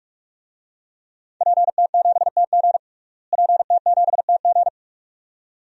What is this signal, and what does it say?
Morse code, heard as a keyed tone, about 42 wpm, 700 Hz. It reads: PT7TG PT7TG
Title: DXCC-100 Call Signs - 5 Characters - Repeated 2x - 45wpm